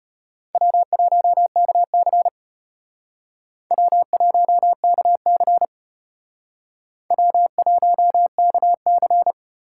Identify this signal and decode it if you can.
Morse code, heard as a keyed tone, about 32 wpm, 700 Hz. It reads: W1KC W1KC W1KC